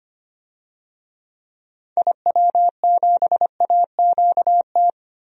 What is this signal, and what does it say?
Morse code, heard as a keyed tone, about 25 wpm, 700 Hz. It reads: IW7AQT
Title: DXCC-100 Call Signs - 6 Characters - Repeated 2x - 25wpm